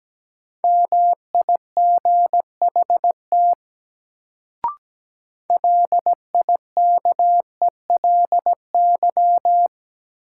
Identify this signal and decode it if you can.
Morse code, heard as a keyed tone, about 17 wpm, 700 Hz. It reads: MIGHT LIKELY
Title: Sets of 2 Words - New Word Drill - 17wpm